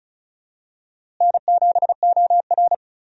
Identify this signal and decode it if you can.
Morse code, heard as a keyed tone, about 36 wpm, 700 Hz. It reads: N7OR
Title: USA Call Signs - 4 Characters - Repeated 3x - 35wpm